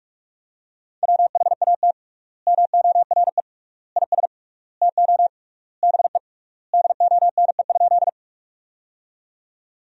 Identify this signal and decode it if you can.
Morse code, heard as a keyed tone, about 41 wpm, 700 Hz. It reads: WHAT MORE IS TO BE DONE?
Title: 3x Word-Spacing - Sentences from Top 500 Words - 45wpm